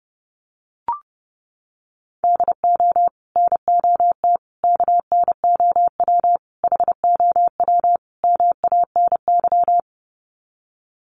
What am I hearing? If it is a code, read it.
Morse code, heard as a keyed tone, about 30 wpm, 700 Hz. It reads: DO NOT KNOW HOW MANY
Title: Sets of 5 Words Encore 30wpm